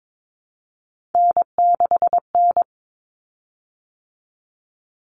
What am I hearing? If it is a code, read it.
Morse code, heard as a keyed tone, about 22 wpm, 700 Hz. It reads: N6N
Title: Common Cut Numbers - 3-Digit 22wpm